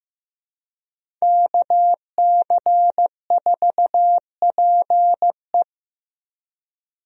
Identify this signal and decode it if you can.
Morse code, heard as a keyed tone, about 15 wpm, 700 Hz. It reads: KC4PE
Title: USA Call Signs - 5 Characters - Repeated 2x - 15wpm